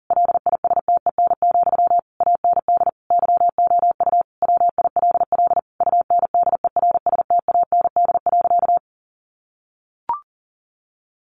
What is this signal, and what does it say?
Morse code, heard as a keyed tone, about 40 wpm, 700 Hz. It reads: LISTEN, AND YOU WILL UNDERSTAND.
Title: Sentences from Top 600 Words 40wpm